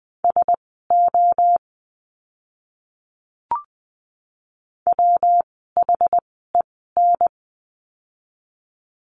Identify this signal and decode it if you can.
Morse code, heard as a keyed tone, about 20 wpm, spaced as for 15 wpm, 700 Hz. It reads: SO WHEN
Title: 230 Lesson 37f - Letter x - Words 20wpm